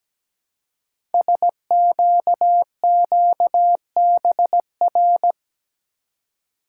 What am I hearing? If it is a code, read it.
Morse code, heard as a keyed tone, about 17 wpm, 700 Hz. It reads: SQQBR
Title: Code Groups - 5 Letters - 17wpm